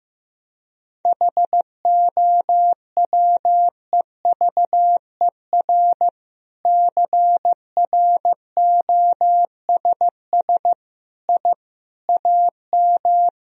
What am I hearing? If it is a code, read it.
Morse code, heard as a keyed tone, about 15 wpm, 700 Hz. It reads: HOWEVER CROSS I AM
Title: Sentences from Top 1000 Words - Review Part 1 - 15wpm